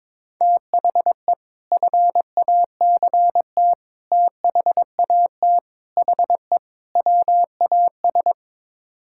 Morse code, heard as a keyed tone, about 22 wpm, 700 Hz. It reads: THE FACT THAT HE WAS
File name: practice / 22wpm / Build A Phrase - 5-Word Phrases - 22wpm